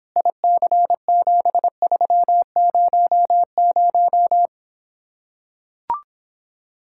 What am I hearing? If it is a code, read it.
Morse code, heard as a keyed tone, about 26 wpm, 700 Hz. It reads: IC7300